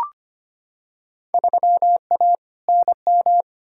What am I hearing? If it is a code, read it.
Morse code, heard as a keyed tone, about 25 wpm, 700 Hz. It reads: 3A NM
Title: Field Day Exchanges Encore - 25wpm